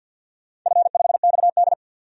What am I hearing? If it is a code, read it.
Morse code, heard as a keyed tone, about 52 wpm, 700 Hz. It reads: F5XD
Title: DXCC-100 Call Signs - 4 Characters - 50wpm